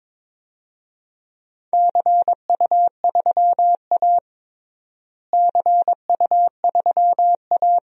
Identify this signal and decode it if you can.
Morse code, heard as a keyed tone, about 22 wpm, 700 Hz. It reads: CU3A CU3A
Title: DXCC-100 Call Signs - 4 Characters - Repeated 2x - 22wpm